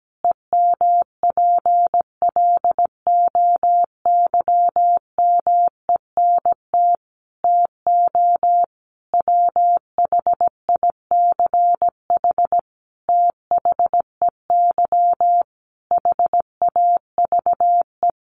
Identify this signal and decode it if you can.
Morse code, heard as a keyed tone, about 17 wpm, 700 Hz. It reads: EMPLOYMENT TO WHICH THEY HAVE
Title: Sentences from Top 1500 Words 17wpm